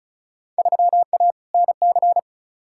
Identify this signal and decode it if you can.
Morse code, heard as a keyed tone, about 36 wpm, 700 Hz. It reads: 3A NC
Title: Field Day Exchanges - 35wpm